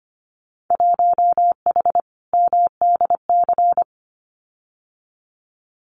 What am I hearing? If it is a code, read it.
Morse code, heard as a keyed tone, about 25 wpm, 690 Hz. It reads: 1H MDC
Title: Winter Field Day Exchanges 25wpm